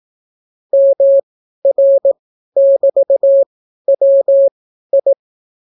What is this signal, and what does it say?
Morse code, heard as a keyed tone, about 18 wpm, spaced as for 12 wpm, 550 Hz. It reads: MR<BT>WI